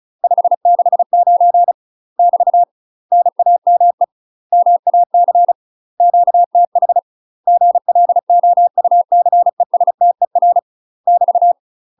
Morse code, heard as a keyed tone, about 36 wpm, 700 Hz. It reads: 569 <BT> NAME MAC QTH GLOUCESTER <BT>
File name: UK MASTER QSOs TRAINING / 35WPM / UK MASTER QSOs 0029